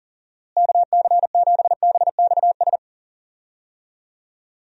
Morse code, heard as a keyed tone, about 40 wpm, 700 Hz. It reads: KC7BXS
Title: USA Call Signs - 6 Characters - Repeated 3x - 40wpm